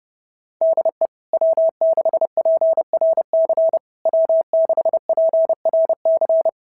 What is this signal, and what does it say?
Morse code, heard as a keyed tone, about 32 wpm, 690 Hz. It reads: DE W6PRC W6PRC
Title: SOTA QSO Practice 30wpm